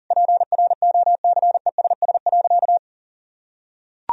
Morse code, heard as a keyed tone, about 40 wpm, 700 Hz. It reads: PROCESS.
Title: Sentences from Top 1000 Words - Review Part 3 - 40wpm